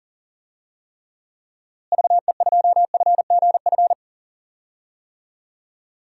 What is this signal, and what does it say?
Morse code, heard as a keyed tone, about 40 wpm, 700 Hz. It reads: VE2FGF
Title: DXCC-100 Call Signs - 6 Characters - Repeated 2x - 40wpm